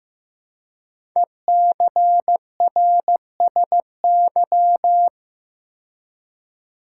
Morse code, heard as a keyed tone, about 15 wpm, 700 Hz. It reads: ECRSY
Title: Code Groups - 5 Letters - 15wpm